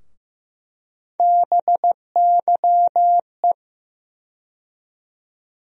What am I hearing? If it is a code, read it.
Morse code, heard as a keyed tone, about 15 wpm, 700 Hz. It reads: BYE
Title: USA Call Sign Postfixes - 3 Letters 15wpm